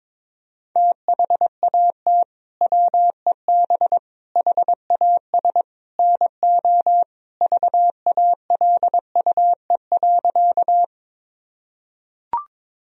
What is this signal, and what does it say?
Morse code, heard as a keyed tone, about 22 wpm, 700 Hz. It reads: THAT WEB HAS NO VALUE.
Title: Sentences from Top 1000 Words - Review Part 4 - 22wpm